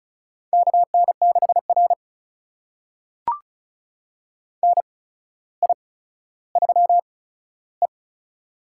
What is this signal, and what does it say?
Morse code, heard as a keyed tone, about 36 wpm, spaced as for 15 wpm, 700 Hz. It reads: KN6R NI3E
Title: USA Call Signs - 4 Characters - 8x Character Spacing - 35wpm